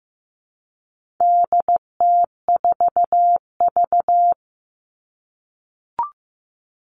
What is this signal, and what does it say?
Morse code, heard as a keyed tone, about 15 wpm, 700 Hz. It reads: DT4V
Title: All Cut Numbers - 4-Digit 15wpm